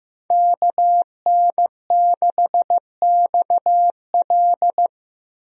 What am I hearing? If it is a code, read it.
Morse code, heard as a keyed tone, about 15 wpm, 700 Hz. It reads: KN6XL
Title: USA Call Signs - 5 Characters - 15wpm